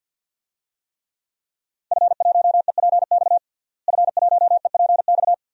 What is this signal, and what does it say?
Morse code, heard as a keyed tone, about 51 wpm, 700 Hz. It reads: F1EPX F1EPX
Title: DXCC-100 Call Signs - 5 Characters - Repeated 2x - 50wpm